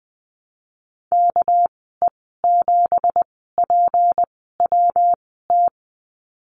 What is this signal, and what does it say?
Morse code, heard as a keyed tone, about 20 wpm, spaced as for 14 wpm, 700 Hz. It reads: KE7PWT
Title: 245 Lesson 39g - Letter z - Call Signs 20wpm